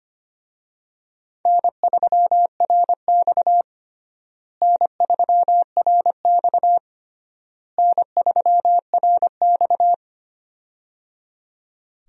Morse code, heard as a keyed tone, about 25 wpm, 700 Hz. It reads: N3RX N3RX N3RX
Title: USA Call Signs - 4 Characters - Repeated 3x - 25wpm